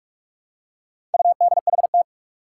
Morse code, heard as a keyed tone, about 44 wpm, 700 Hz. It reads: UDHT
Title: Four Challenging Letter-Number Combinations 45wpm